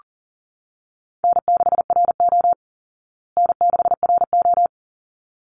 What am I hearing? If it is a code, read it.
Morse code, heard as a keyed tone, about 37 wpm, 700 Hz. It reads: N6RO N6RO
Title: Call Signs Repeated 3x - 40wpm